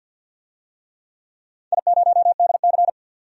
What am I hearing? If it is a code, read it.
Morse code, heard as a keyed tone, about 53 wpm, 700 Hz. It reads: I0DC
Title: DXCC-100 Call Signs - 4 Characters - 50wpm